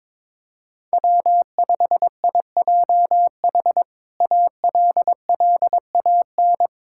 Morse code, heard as a keyed tone, about 22 wpm, 700 Hz. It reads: W5IJH ALLAN
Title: K1USN SST QSO Practice - 22wpm